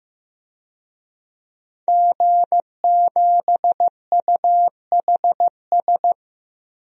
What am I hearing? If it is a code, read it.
Morse code, heard as a keyed tone, about 15 wpm, 700 Hz. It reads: G7UHS
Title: DXCC-100 Call Signs - 5 Characters - Repeated 3x - 15wpm